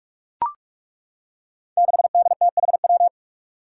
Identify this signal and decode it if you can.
Morse code, heard as a keyed tone, about 41 wpm, 700 Hz. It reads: 6DTHW
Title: Five Challenging Letter-Number Combinations 45wpm